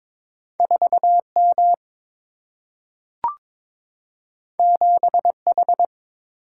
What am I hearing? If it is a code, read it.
Morse code, heard as a keyed tone, about 22 wpm, 700 Hz. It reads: 4 M 7 H
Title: Single-Letter Single-Number 22wpm